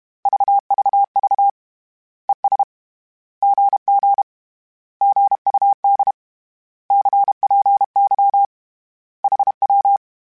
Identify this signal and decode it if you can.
Morse code, heard as a keyed tone, about 31 wpm, 800 Hz. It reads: VVV ES GG GUD CPY HW
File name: Misc / Misc-mp3s / Words @30